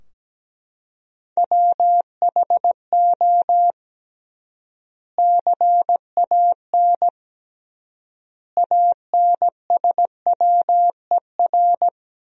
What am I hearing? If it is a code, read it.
Morse code, heard as a keyed tone, about 17 wpm, 700 Hz. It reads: WHO CAN ANSWER
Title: 3x Word-Spacing - Sentences from Top 300 Words - 17wpm